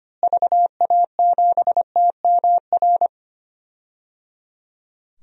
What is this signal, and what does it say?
Morse code, heard as a keyed tone, about 25 wpm, 700 Hz. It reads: VA7TMR
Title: DXCC-100 Call Signs - 6 Characters - Repeated 3x - 25wpm